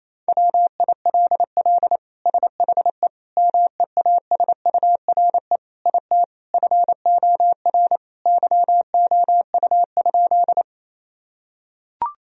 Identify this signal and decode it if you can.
Morse code, heard as a keyed tone, about 28 wpm, 700 Hz. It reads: WILL SHE MEASURE IT FOR YOU?
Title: Sentences from Top 1000 Words - Review Part 2 - 28wpm